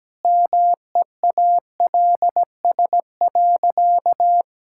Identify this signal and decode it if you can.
Morse code, heard as a keyed tone, about 17 wpm, 700 Hz. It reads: MEALS.
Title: Pangram Challenge v2 - 17wpm